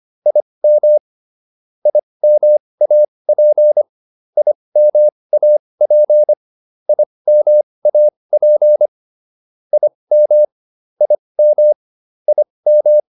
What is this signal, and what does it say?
Morse code, heard as a keyed tone, about 25 wpm, spaced as for 20 wpm, 600 Hz. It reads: IM IMAP IMAP IMAP IM IM IM